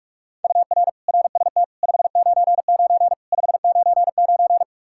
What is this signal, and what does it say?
Morse code, heard as a keyed tone, about 46 wpm, 700 Hz. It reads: UR RST 599 599
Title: Practice QSO - Exchange 1 - Beginning - 45wpm